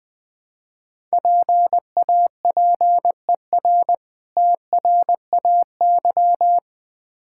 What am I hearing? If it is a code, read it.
Morse code, heard as a keyed tone, about 23 wpm, 700 Hz. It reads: PAPER TRAY
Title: Difficult Words - Speed-Racing with Hints - 20wpm